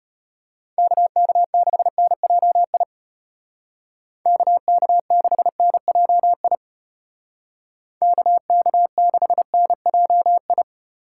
Morse code, heard as a keyed tone, about 34 wpm, 700 Hz. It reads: KK6NJI KK6NJI KK6NJI